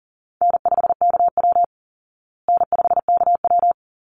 Morse code, heard as a keyed tone, about 40 wpm, 700 Hz. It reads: N5KW N5KW
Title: Call Signs Repeated 3x - 40wpm